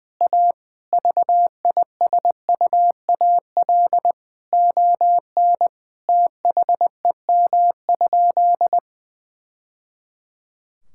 Morse code, heard as a keyed tone, about 20 wpm, 700 Hz. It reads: A VISUAL ON THEM?
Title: Sentences from Top 1000 Words - Review Part 6 - 20wpm